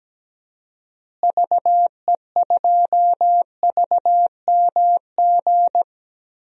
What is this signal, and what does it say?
Morse code, heard as a keyed tone, about 17 wpm, 700 Hz. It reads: VE2VMG
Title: DXCC-100 Call Signs - 6 Characters - Repeated 2x - 17wpm